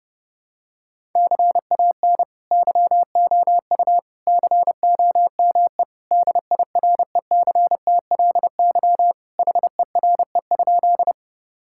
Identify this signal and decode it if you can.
Morse code, heard as a keyed tone, about 29 wpm, 700 Hz. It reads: CAN YOU COME DIRECTLY HERE?
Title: Sentences from Top 1000 Words - Review Part 5 - 30wpm